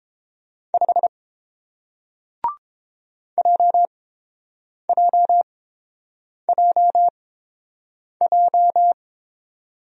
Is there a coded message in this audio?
Yes: Morse code from 0.7 to 8.9 s, heard as a keyed tone, about 30 wpm, spaced as for 7 wpm, 700 Hz. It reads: 5 JJJJ